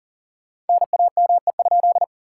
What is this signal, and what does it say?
Morse code, heard as a keyed tone, about 54 wpm, spaced as for 47 wpm, 700 Hz. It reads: NAME?
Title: Speed-Racing Common QSO Phrases - 40wpm